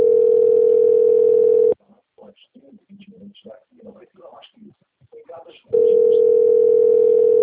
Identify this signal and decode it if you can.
Ringback tone (North America), 440 + 480 Hz, with speech.
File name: rg-1600-unknown- - - .93361